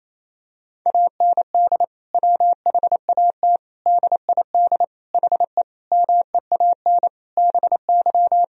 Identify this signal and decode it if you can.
Morse code, heard as a keyed tone, about 27 wpm, 700 Hz. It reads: AND WHAT DID HE MEAN BY
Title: Sentences from Top 1000 Words - Review Part 3 - 28wpm